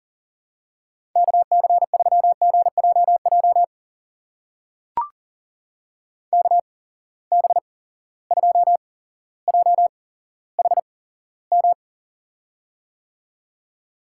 Morse code, heard as a keyed tone, about 43 wpm, spaced as for 17 wpm, 700 Hz. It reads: KC3GJJ KB2JHM
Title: USA Call Signs - 6 Characters - 8x Character Spacing - 40wpm